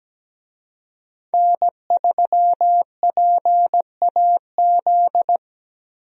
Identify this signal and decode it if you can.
Morse code, heard as a keyed tone, about 17 wpm, 700 Hz. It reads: N3PAZ
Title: USA Call Signs - 5 Characters - 17wpm